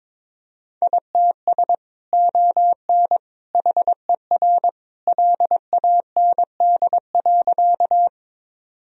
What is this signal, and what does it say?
Morse code, heard as a keyed tone, about 22 wpm, 700 Hz. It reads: ITS ON HER LAND.
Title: Sentences from Top 1000 Words - Review Part 1 - 22wpm